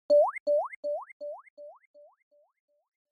There is no code.